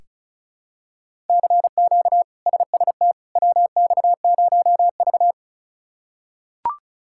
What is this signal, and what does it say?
Morse code, heard as a keyed tone, about 34 wpm, 700 Hz. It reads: CQ SST WX0V
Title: K1USN SST QSO Practice - 35wpm